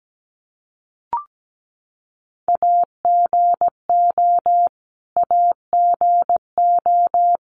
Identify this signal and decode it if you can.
Morse code, heard as a keyed tone, about 17 wpm, 700 Hz. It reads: AGO AGO
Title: Top 500 Words Repeated 17wpm